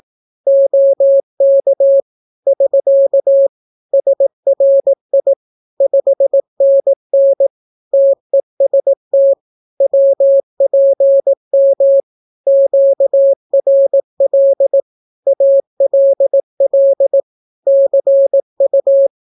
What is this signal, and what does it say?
Morse code, heard as a keyed tone, about 18 wpm, 550 Hz. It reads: OK <SK> SRI 5NN TEST WPM QRL ALL CU
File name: naukapodstaw / skroty / skroty